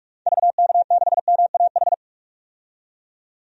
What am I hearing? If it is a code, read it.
Morse code, heard as a keyed tone, about 46 wpm, 700 Hz. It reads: VK6MAH
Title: DXCC-100 Call Signs - 6 Characters - 45wpm